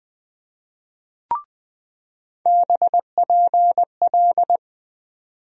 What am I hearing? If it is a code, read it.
Morse code, heard as a keyed tone, about 20 wpm, 700 Hz. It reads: BPL
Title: USA Call Sign Postfixes - 3 Letters 20wpm